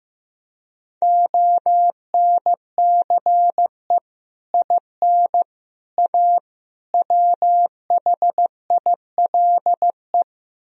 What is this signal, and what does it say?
Morse code, heard as a keyed tone, about 15 wpm, 700 Hz. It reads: ONCE IN A WHILE
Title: Speed-Racing - Sets of 4 Words 15wpm